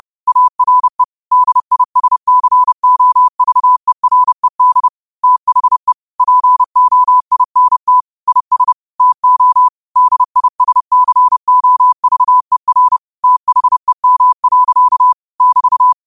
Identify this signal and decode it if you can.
Morse code, heard as a keyed tone, about 30 wpm, 1000 Hz. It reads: ARE DISCOVERED THE POINT IS TO DISCOVER THEM. <BT>
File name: qotdfiles / 30wpm